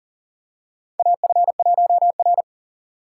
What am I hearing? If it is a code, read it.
Morse code, heard as a keyed tone, about 43 wpm, 690 Hz. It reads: AF1R
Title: USA Call Signs - 4 Characters - 6x Character Spacing - 40wpm